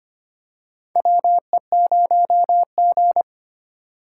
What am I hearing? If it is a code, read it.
Morse code, heard as a keyed tone, about 25 wpm, 700 Hz. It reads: WE0G
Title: USA Call Signs - 4 Characters - 3x Character Spacing - 25wpm